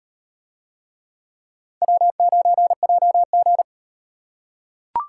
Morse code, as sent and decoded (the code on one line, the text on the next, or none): .-- ----. .--- --.
W9JG